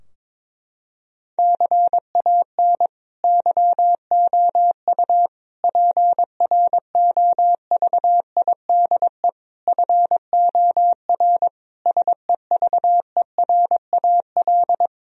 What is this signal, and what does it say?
Morse code, heard as a keyed tone, about 22 wpm, 700 Hz. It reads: CAN YOU PROVIDE FOR SEVERAL
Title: Sentences from Top 1000 Words - Review Part 2 - 22wpm